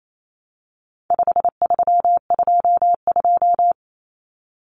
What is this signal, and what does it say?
Morse code, heard as a keyed tone, about 28 wpm, 700 Hz. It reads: 5322
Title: Common Cut Numbers - 4-Digit 28wpm